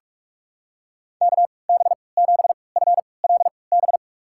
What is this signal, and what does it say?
Morse code, heard as a keyed tone, about 47 wpm, spaced as for 26 wpm, 700 Hz. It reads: KB7FLB
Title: USA Call Signs - 3x Character Spacing - 45wpm